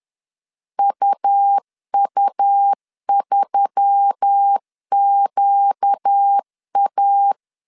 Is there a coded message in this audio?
Yes: Morse code from 0.8 to 7.3 s, heard as a keyed tone, about 10 wpm, 800 Hz. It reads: UU3QA